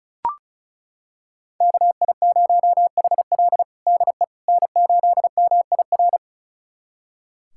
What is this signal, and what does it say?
Morse code, heard as a keyed tone, about 36 wpm, 700 Hz. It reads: KI0HL DE N8MIR
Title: USA Callsign DE Callsign - 5 Characters - 35wpm